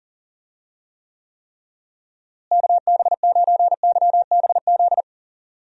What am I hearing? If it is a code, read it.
Morse code, heard as a keyed tone, about 40 wpm, 700 Hz. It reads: KB9YBZ
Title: USA Call Signs - 6 Characters - Repeated 3x - 40wpm